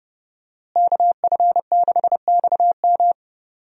Morse code, heard as a keyed tone, about 30 wpm, 700 Hz. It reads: KF6XM
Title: USA Call Signs - 5 Characters - Repeated 3x - 30wpm